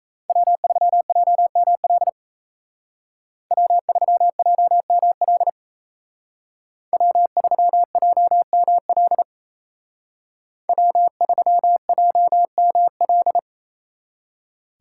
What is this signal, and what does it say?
Morse code, heard as a keyed tone, about 36 wpm, 700 Hz. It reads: W3JML W3JML W3JML W3JML